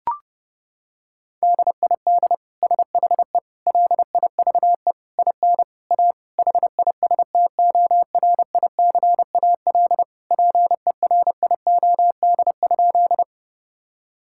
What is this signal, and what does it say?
Morse code, heard as a keyed tone, about 30 wpm, 700 Hz. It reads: DID SHE LIVE IN A HISTORICAL PERIOD?
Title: Sentences from Top 1000 Words - Review Part 5 - 30wpm